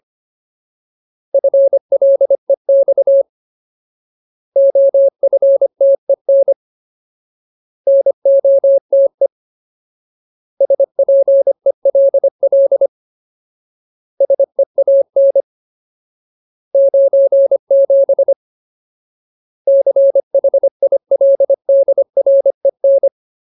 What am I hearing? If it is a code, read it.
Morse code, heard as a keyed tone, about 25 wpm, 550 Hz. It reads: FLEX OFTEN NOTE SPELL SEAN 97 CHILDREN
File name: Ultimate Word List / 25WPM Ultimate